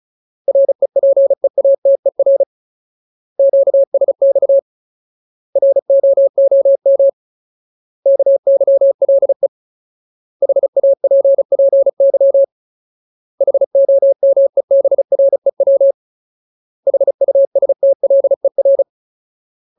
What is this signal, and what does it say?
Morse code, heard as a keyed tone, about 34 wpm, 550 Hz. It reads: REPEATER QSX ROOM KYLE HAPPY HOMEBREW HUSTLER
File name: Ultimate Word List / 35WPM Ultimate